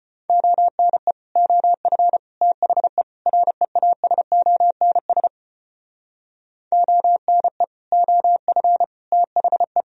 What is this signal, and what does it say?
Morse code, heard as a keyed tone, about 32 wpm, 700 Hz. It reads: ONE OF THE REASONS ONE OF THE